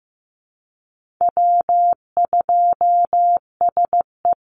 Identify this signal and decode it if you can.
Morse code, heard as a keyed tone, about 15 wpm, 700 Hz. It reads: W2SE
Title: NA SOTA Chaser Callsigns 15wpm